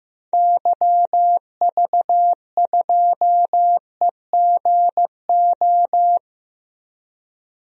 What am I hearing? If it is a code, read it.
Morse code, heard as a keyed tone, about 15 wpm, 700 Hz. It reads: YV2EGO
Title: DXCC-100 Call Signs - 6 Characters - 15wpm